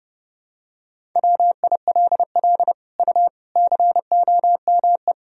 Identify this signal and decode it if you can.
Morse code, heard as a keyed tone, about 30 wpm, 700 Hz. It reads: WILL U COME
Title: Sentences from Top 200 Words and Abbreviations 30wpm